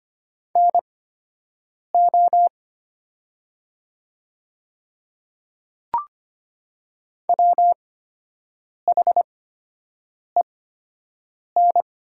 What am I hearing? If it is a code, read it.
Morse code, heard as a keyed tone, about 25 wpm, spaced as for 7 wpm, 700 Hz. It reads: NO WHEN
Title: 8x Character Spacing - Top 100 Words - 25wpm